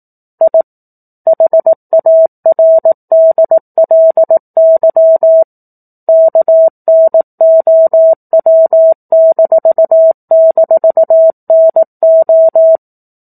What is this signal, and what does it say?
Morse code, heard as a keyed tone, about 18 wpm, 650 Hz. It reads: I HARDLY KNOW--NO
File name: 04-Chapter-04 18 WPM